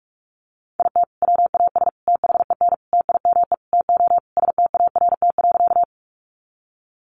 Morse code, heard as a keyed tone, about 44 wpm, 700 Hz. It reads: IT WAS THEN TIME TO START.
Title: Sentences from Top 300 Words 45wpm